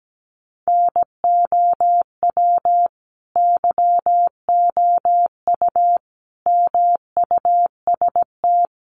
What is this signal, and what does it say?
Morse code, heard as a keyed tone, about 17 wpm, 700 Hz. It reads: NOW YOU MUST
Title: Sentences from Top 300 Words 17wpm